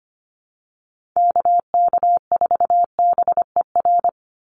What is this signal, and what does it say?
Morse code, heard as a keyed tone, about 25 wpm, 700 Hz. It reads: KK4BER
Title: USA Call Signs 25wpm